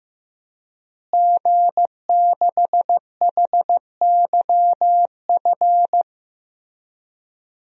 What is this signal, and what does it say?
Morse code, heard as a keyed tone, about 15 wpm, 700 Hz. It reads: G6HYF
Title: DXCC-100 Call Signs - 5 Characters - Repeated 3x - 15wpm